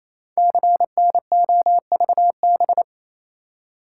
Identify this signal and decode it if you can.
Morse code, heard as a keyed tone, about 28 wpm, 700 Hz. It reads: CNOVB
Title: Code Groups - 5 Letters - 28wpm